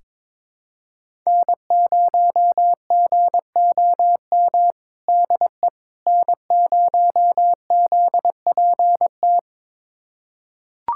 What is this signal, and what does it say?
Morse code, heard as a keyed tone, about 22 wpm, 700 Hz. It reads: N0GOM DE N0ZPT
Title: USA Callsign DE Callsign - 5 Characters - 22wpm